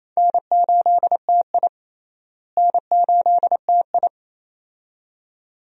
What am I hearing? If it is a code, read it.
Morse code, heard as a keyed tone, about 27 wpm, 700 Hz. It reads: N8TI N8TI
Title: USA Call Signs - 4 Characters - Repeated 2x - 28wpm